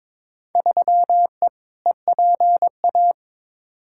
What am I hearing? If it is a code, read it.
Morse code, heard as a keyed tone, about 22 wpm, 700 Hz. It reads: 3E EPA
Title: Field Day Exchanges Encore - 22wpm